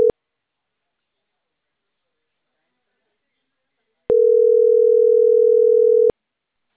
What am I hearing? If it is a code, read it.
Ringback tone (North America), 440 + 480 Hz.